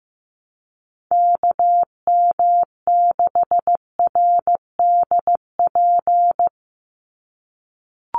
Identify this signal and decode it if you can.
Morse code, heard as a keyed tone, about 15 wpm, 700 Hz. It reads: KM6RDP
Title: USA Call Signs 15wpm